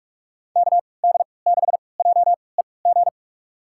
Morse code, heard as a keyed tone, about 46 wpm, spaced as for 26 wpm, 700 Hz. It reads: KD6JEG
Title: USA Call Signs - 6 Characters - 3x Character Spacing - 45wpm